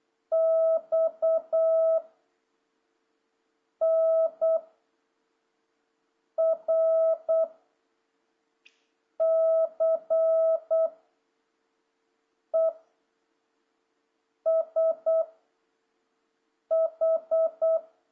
Morse code, heard as a keyed tone, about 8 wpm, 660 Hz. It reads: X N R C E S H
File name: jh8cbh / 2026cw